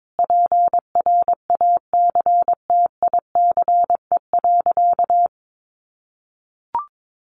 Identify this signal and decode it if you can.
Morse code, heard as a keyed tone, about 22 wpm, 700 Hz. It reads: PRACTICE.
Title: Sentences from Top 900 Words 22wpm